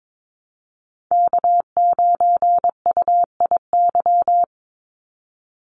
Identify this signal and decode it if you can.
Morse code, heard as a keyed tone, about 22 wpm, 700 Hz. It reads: K9UIY
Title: Call Signs - 22wpm